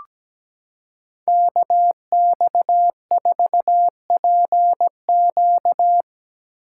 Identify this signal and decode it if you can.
Morse code, heard as a keyed tone, about 17 wpm, 700 Hz. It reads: KX4PQ
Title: K1USN SST QSO Practice - 17wpm